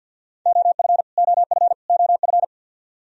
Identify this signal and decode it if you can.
Morse code, heard as a keyed tone, about 47 wpm, 700 Hz. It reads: OF OF OF
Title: Top 100 Words Repeated 50wpm